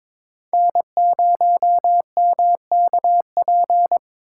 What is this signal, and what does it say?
Morse code, heard as a keyed tone, about 26 wpm, 700 Hz. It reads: N0MKP
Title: Speed Racing - USA 1x3 Call Signs 22wpm